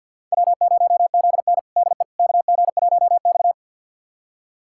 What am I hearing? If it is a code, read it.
Morse code, heard as a keyed tone, about 52 wpm, 700 Hz. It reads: W0ZN DE KG1X
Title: USA Callsign DE Callsign - 4 Characters - 50wpm